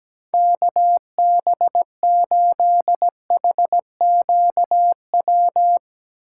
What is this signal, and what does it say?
Morse code, heard as a keyed tone, about 17 wpm, 700 Hz. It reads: KB8HQW
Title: USA Call Signs - 6 Characters - 8x Character Spacing - 17wpm